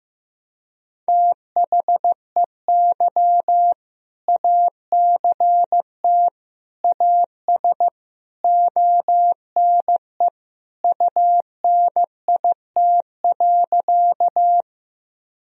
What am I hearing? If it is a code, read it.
Morse code, heard as a keyed tone, about 15 wpm, 700 Hz. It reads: THEY ACT AS ONE UNIT.
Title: Sentences from Top 1000 Words - Review Part 2 - 15wpm